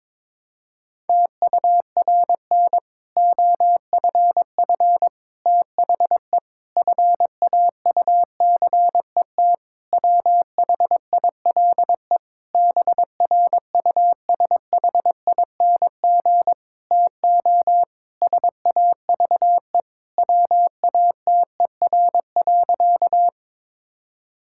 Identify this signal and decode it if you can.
Morse code, heard as a keyed tone, about 22 wpm, 700 Hz. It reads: TURN OFF THE FAUCET WHILE BRUSHING TO SAVE WATER.
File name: Difficult Sentences with Hints - 22wpm